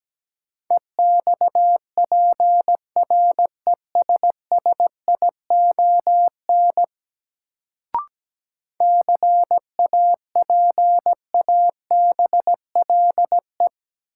Morse code, heard as a keyed tone, about 17 wpm, 700 Hz. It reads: EXPRESSION CAPABLE
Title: Sentences from Top 1200 Words - New Word Drill - 17wpm